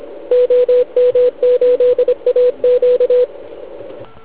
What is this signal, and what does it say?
Morse code, heard as a keyed tone, about 26 wpm, 500 Hz. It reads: OM8AQ
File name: Archiv / CUC149 / OK1IF CUC149